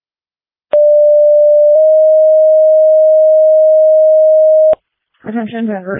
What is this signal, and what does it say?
Two-tone page, heard as a one-second tone, then a three-second tone, about 590 then 620 Hz, with speech. It is heard from 0.7 to 4.7 s.